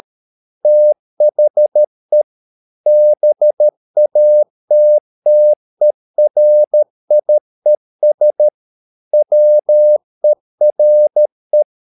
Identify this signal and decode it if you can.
Morse code, heard as a keyed tone, about 13 wpm, 600 Hz. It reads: THE BATTERIES WERE
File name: mp3 files / Lexiphiles - 13wpm